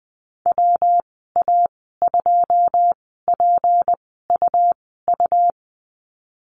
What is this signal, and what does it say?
Morse code, heard as a keyed tone, about 20 wpm, spaced as for 14 wpm, 700 Hz. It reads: WA2PUU
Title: 161 Lesson 26g - Letter g - Call Signs 20wpm